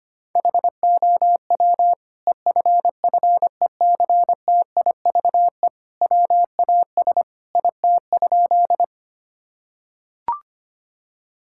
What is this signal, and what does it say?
Morse code, heard as a keyed tone, about 25 wpm, 700 Hz. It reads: HOW EFFECTIVE WAS IT?
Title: Sentences from Top 1000 Words - Review Part 6 - 25wpm